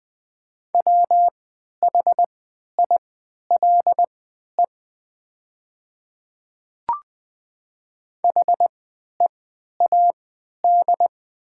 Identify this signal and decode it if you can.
Morse code, heard as a keyed tone, about 20 wpm, spaced as for 11 wpm, 690 Hz. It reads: WHILE HEAD
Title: 3x Character Spacing - Top 300 Words - 20wpm